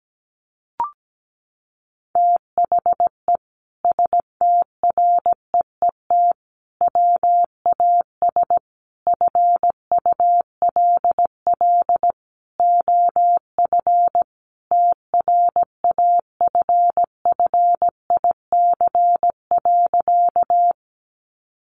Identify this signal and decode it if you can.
Morse code, heard as a keyed tone, about 17 wpm, 700 Hz. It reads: THE STREET WAS FULL OF TRAFFIC.
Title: Sentences from Top 1300 Words 17wpm